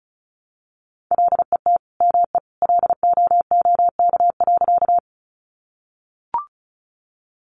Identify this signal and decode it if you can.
Morse code, heard as a keyed tone, about 34 wpm, 700 Hz. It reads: LET ME LOOK.
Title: Sentences from Top 300 Words 35wpm